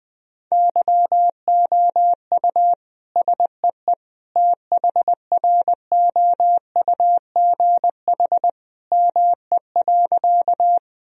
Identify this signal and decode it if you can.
Morse code, heard as a keyed tone, about 21 wpm, 700 Hz. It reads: YOU SEE THROUGH ME.
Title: Sentences from Top 1000 Words - Review Part 1 - 20wpm